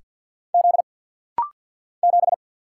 Morse code, heard as a keyed tone, about 50 wpm, 700 Hz. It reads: Z 7
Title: Single Letter-Number - Rapid-Fire 50wpm